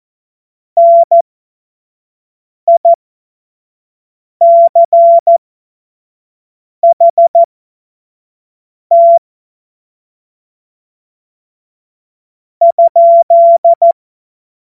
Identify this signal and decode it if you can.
Morse code, heard as a keyed tone, about 14 wpm, spaced as for 4 wpm, 680 Hz. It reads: NICHT?
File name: Morsekurs / Witze / Witz 30. Mai 2023 25 F70